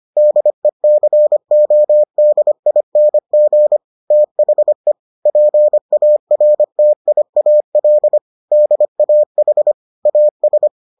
Morse code, heard as a keyed tone, about 25 wpm, 600 Hz. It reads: DECODING THE PARTIAL DAH AS